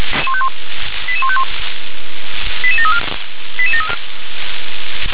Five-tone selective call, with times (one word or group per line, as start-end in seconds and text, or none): 1.1-1.4 88151
2.6-3.0 88734